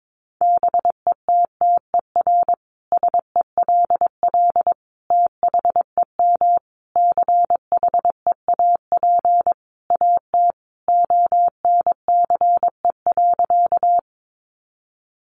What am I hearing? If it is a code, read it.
Morse code, heard as a keyed tone, about 22 wpm, 700 Hz. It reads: BETTER SELL THEM CHEAP AT ONCE.
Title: Sentences from Top 1200 Words 22wpm